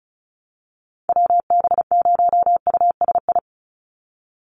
Morse code, heard as a keyed tone, about 36 wpm, 700 Hz. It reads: WB0USI
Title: NA SOTA Chaser Callsigns 35wpm